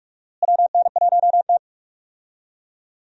Morse code, heard as a keyed tone, about 42 wpm, 700 Hz. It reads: WN1T
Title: USA Call Signs - 4 Characters - Repeated 3x - 45wpm